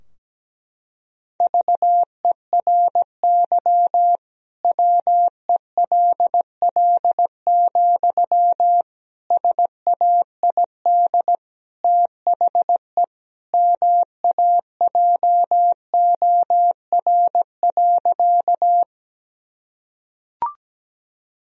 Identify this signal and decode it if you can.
Morse code, heard as a keyed tone, about 17 wpm, 700 Hz. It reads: VERY WELL, SAID THE MAJOR.
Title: Sentences from Top 1000 Words - Review Part 4 - 17wpm